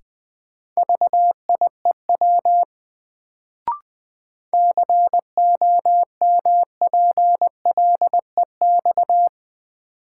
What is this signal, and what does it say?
Morse code, heard as a keyed tone, about 21 wpm, 700 Hz. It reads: VIEW COMPLEX
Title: Sentences from Top 900 Words - New Word Drill - 20wpm